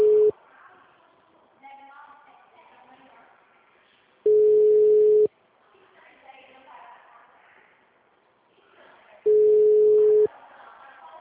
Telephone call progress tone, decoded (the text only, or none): ringback tone